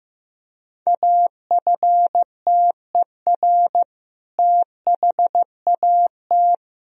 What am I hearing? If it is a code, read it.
Morse code, heard as a keyed tone, about 16 wpm, 700 Hz. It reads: AFTER THAT
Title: Sentences from Top 1000 Words - Review Part 1 - 15wpm